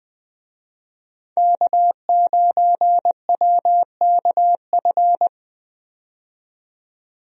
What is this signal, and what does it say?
Morse code, heard as a keyed tone, about 20 wpm, 700 Hz. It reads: K9WKF
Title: USA Call Signs - 5 Characters - 20wpm